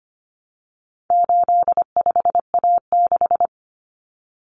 Morse code, heard as a keyed tone, about 25 wpm, 700 Hz. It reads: 85A6
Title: Common Cut Numbers - 4-Digit 25wpm